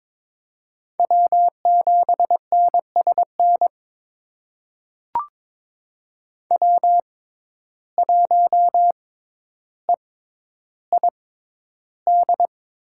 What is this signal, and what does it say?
Morse code, heard as a keyed tone, about 22 wpm, spaced as for 11 wpm, 700 Hz. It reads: W7NSN W1EID
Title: USA Call Signs - 5 Characters - 6x Character Spacing - 22wpm